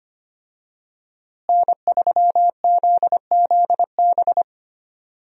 Morse code, heard as a keyed tone, about 25 wpm, 700 Hz. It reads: N3ZZB
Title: USA Call Signs - 5 Characters - Repeated 3x - 25wpm